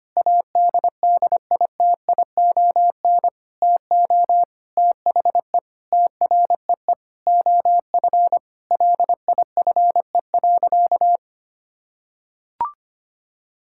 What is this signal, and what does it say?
Morse code, heard as a keyed tone, about 25 wpm, 700 Hz. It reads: ADDITION TO THE TREE OF LIFE.
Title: Sentences from Top 1000 Words - Review Part 6 - 25wpm